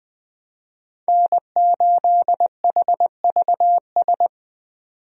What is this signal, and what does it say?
Morse code, heard as a keyed tone, about 20 wpm, 700 Hz. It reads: N8HVS
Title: USA Call Signs - 5 Characters - 20wpm